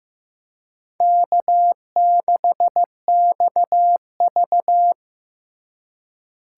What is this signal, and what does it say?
Morse code, heard as a keyed tone, about 15 wpm, 700 Hz. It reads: K6XV
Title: USA Call Signs - 4 Characters - 2x Character Spacing - 15wpm